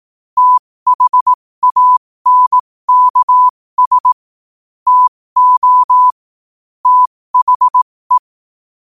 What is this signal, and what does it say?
Morse code, heard as a keyed tone, about 18 wpm, spaced as for 15 wpm, 1000 Hz. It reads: THANKS TO THE
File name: qotdfiles / 15wpm